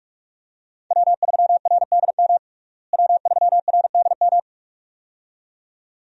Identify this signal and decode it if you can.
Morse code, heard as a keyed tone, about 44 wpm, 700 Hz. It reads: W3RDM W3RDM
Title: USA Call Signs - 5 Characters - Repeated 2x - 45wpm